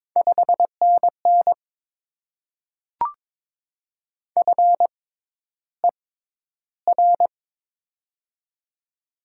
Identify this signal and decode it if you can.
Morse code, heard as a keyed tone, about 22 wpm, spaced as for 11 wpm, 700 Hz. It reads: 5NN FER
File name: Top 20 QSO Elements - 6x Character Spacing - 22wpm